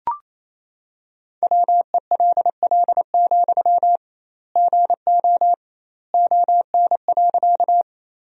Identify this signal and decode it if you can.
Morse code, heard as a keyed tone, about 28 wpm, 700 Hz. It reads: WELL, GO ON.
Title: 2x Word-Spacing - Sentences from Top 200 Words - 28wpm